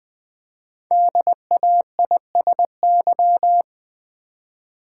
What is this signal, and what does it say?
Morse code, heard as a keyed tone, about 20 wpm, 700 Hz. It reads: DAISY
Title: Easy Words with Hints - 20wpm